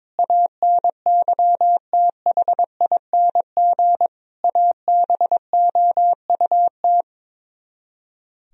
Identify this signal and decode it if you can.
Morse code, heard as a keyed tone, about 22 wpm, 700 Hz. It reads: ANYTHING ABOUT
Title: Speed-Racing - Sets of 5 Words 22wpm